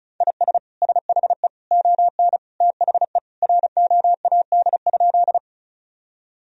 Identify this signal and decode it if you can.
Morse code, heard as a keyed tone, about 36 wpm, 700 Hz. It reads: IS SHE ON THE ROAD?
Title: Sentences from Top 1000 Words - Review Part 2 - 35wpm